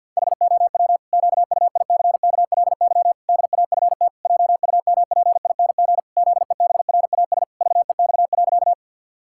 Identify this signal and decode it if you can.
Morse code, heard as a keyed tone, about 50 wpm, 700 Hz. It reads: HOW QUICKLY DAFT JUMPING ZEBRAS VEX.
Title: Pangram Challenge v2 - 50wpm